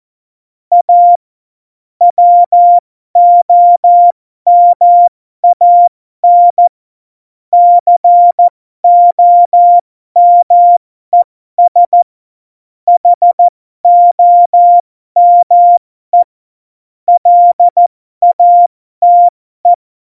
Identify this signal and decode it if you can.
Morse code, heard as a keyed tone, about 14 wpm, spaced as for 12 wpm, 700 Hz. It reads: A WOMAN COMES HOME LATE
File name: Morsekurs / Witze